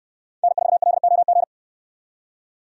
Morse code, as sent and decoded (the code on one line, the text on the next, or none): -. ....- .-.. --- --.
N4LOG